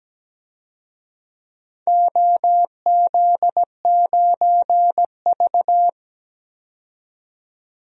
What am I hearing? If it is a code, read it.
Morse code, heard as a keyed tone, about 17 wpm, 700 Hz. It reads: OZ9V
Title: DXCC-100 Call Signs - 4 Characters - 17wpm